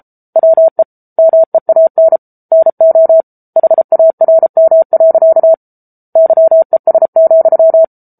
Morse code, heard as a keyed tone, about 33 wpm, 650 Hz. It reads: WE MEAN NO HARM. YES,
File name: Book Two / 07-Chapter 7 - 33 WPM